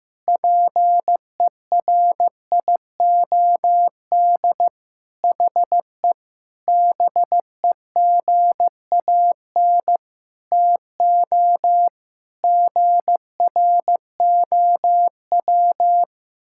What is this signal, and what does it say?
Morse code, heard as a keyed tone, about 15 wpm, 700 Hz. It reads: PERIOD HE BEGAN TO GROW
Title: Sentences from Top 1000 Words - Review Part 5 - 15wpm